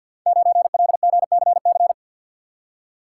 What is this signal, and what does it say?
Morse code, heard as a keyed tone, about 51 wpm, 700 Hz. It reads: 9LGCC
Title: Code Groups - 5 Letter:Numbers - 50wpm